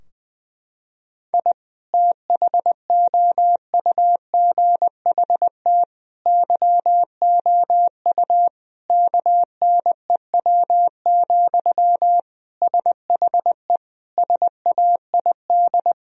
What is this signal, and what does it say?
Morse code, heard as a keyed tone, about 20 wpm, 700 Hz. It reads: I THOUGHT YOU KNEW, SHE SAID
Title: Sentences from Top 1000 Words - Review Part 4 - 20wpm